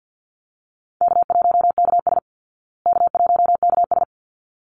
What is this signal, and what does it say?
Morse code, heard as a keyed tone, about 50 wpm, 700 Hz. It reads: K1KS K1KS
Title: Easy Call Signs Repeated 2x - 50wpm